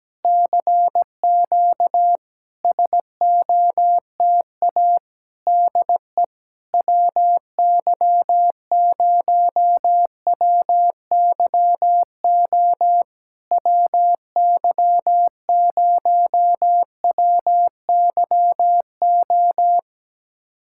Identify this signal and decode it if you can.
Morse code, heard as a keyed tone, about 17 wpm, 700 Hz. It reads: CQ SOTA DE WY0WYO WY0WYO
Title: SOTA QSO Practice 17wpm